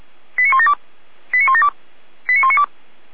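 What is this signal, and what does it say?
Five-tone selective call (ZVEI), heard as quick runs of five tones, tones of about 70 ms. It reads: 89172 89172 89192
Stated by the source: ZVEI five-tone call received on an SDR.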